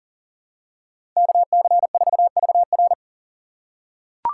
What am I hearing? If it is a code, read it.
Morse code, heard as a keyed tone, about 40 wpm, 700 Hz. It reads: KC4VR
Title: USA Call Signs - 5 Characters - 3x Character Spacing - 40wpm